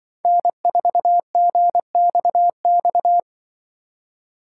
Morse code, heard as a keyed tone, about 25 wpm, 700 Hz. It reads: N4GXX
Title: Speed Racing - USA 1x3 Call Signs 20wpm